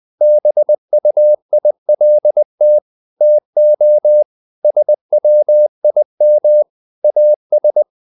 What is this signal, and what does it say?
Morse code, heard as a keyed tone, about 20 wpm, 600 Hz. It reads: BUILT TO SWIM AS
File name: the old man and the sea part 2 20wpm